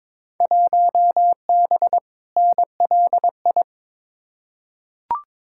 Morse code, heard as a keyed tone, about 22 wpm, 700 Hz. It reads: 1B NLI
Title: Field Day Exchanges - 22wpm